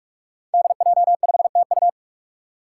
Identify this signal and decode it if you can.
Morse code, heard as a keyed tone, about 44 wpm, 700 Hz. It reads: DJ5TU
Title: Five Challenging Letter-Number Combinations 45wpm